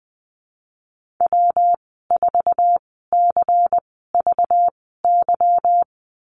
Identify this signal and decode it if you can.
Morse code, heard as a keyed tone, about 19 wpm, spaced as for 14 wpm, 700 Hz. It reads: W4CVY
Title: 193 Lesson 31g - Letter v - Call Signs 20wpm